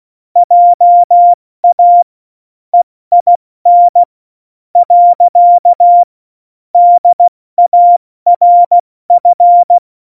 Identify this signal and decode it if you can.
Morse code, heard as a keyed tone, about 16 wpm, spaced as for 13 wpm, 700 Hz. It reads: JA EIN. DARF
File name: Morsekurs / Witze / Witz 13. Oktober 2020, 70 F80